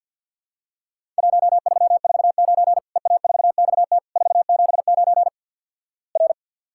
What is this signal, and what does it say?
Morse code, heard as a keyed tone, about 50 wpm, 700 Hz. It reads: 1349 EA4XT 479 R
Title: Worked All Europe DX Contest - QTC Exchange - 50wpm